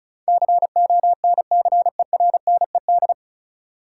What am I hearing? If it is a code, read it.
Morse code, heard as a keyed tone, about 36 wpm, 700 Hz. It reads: CONCERNED
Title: Sentences from Top 1000 Words - Review Part 6 - New Word Drill - 35wpm